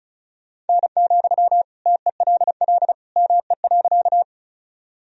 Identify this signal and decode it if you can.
Morse code, heard as a keyed tone, about 36 wpm, 700 Hz. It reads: N, TELL ME.
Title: Sentences from Top 200 Words and Abbreviations 35wpm